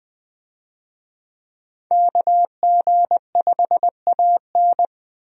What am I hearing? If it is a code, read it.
Morse code, heard as a keyed tone, about 20 wpm, 700 Hz. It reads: KG5AN
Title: USA Call Signs - 5 Characters - Repeated 2x - 20wpm